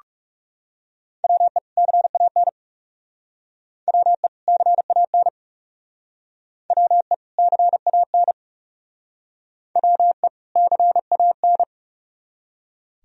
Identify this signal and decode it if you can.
Morse code, heard as a keyed tone, about 37 wpm, 700 Hz. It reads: WE CAN WE CAN WE CAN WE CAN
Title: Speed-Racing - Sets of 2 Words 30wpm